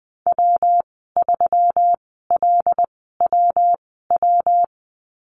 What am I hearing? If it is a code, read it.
Morse code, heard as a keyed tone, about 20 wpm, spaced as for 15 wpm, 700 Hz. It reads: W3LWW
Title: 154 Lesson 25g - Letter p - Call Signs 20wpm